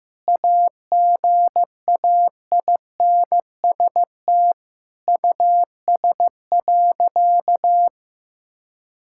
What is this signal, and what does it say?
Morse code, heard as a keyed tone, about 15 wpm, 700 Hz. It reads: AGAINST US.
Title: Sentences from Top 1000 Words - Review Part 6 - 15wpm